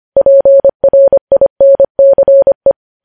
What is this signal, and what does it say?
Morse code, heard as a keyed tone, about 25 wpm, 550 Hz. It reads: PRINCE
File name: podcast / News headlines-1x-25x25